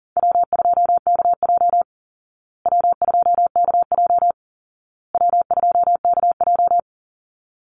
Morse code, heard as a keyed tone, about 40 wpm, 700 Hz. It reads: W2KJ W2KJ W2KJ
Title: Easy Call Signs Repeated 3x - 40wpm